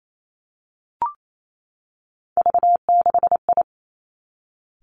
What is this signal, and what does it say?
Morse code, heard as a keyed tone, about 28 wpm, 700 Hz. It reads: V6I
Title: Three Challenging Letter-Number Combinations 28wpm